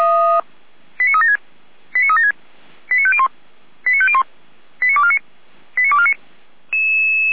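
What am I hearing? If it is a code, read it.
Five-tone selective call (ZVEI), heard as quick runs of five tones, tones of about 70 ms. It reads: 89376 89376 89501 89501 89249 89249
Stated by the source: ZVEI five-tone selective call received on an SDR.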